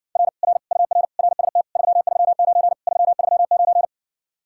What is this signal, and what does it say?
Morse code, heard as a keyed tone, about 52 wpm, 700 Hz. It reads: R R UR RST 339 339